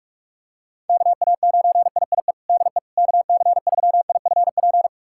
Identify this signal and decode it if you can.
Morse code, heard as a keyed tone, about 41 wpm, 700 Hz. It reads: KA9IIE DE KC3IFP
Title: USA Callsign DE Callsign 45wpm